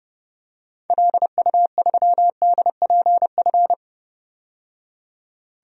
Morse code, heard as a keyed tone, about 32 wpm, 700 Hz. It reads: LU3DPF
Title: DXCC-100 Call Signs - 6 Characters - 30wpm